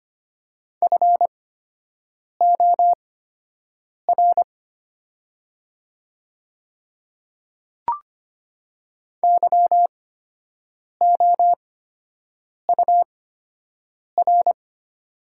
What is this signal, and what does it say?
Morse code, heard as a keyed tone, about 25 wpm, spaced as for 7 wpm, 700 Hz. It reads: FOR YOUR
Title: 8x Character Spacing - Top 100 Words - 25wpm